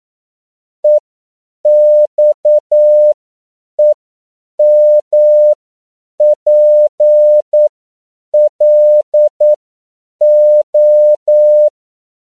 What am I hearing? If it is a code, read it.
Morse code, heard as a keyed tone, about 9 wpm, spaced as for 7 wpm, 600 Hz. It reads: EXEMPLO
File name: CW / 10PPM